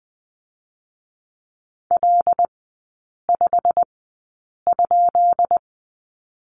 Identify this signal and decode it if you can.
Morse code, heard as a keyed tone, about 20 wpm, spaced as for 8 wpm, 690 Hz. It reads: L5?
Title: 133 Lesson 22e - Question Mark - Three Characters 20wpm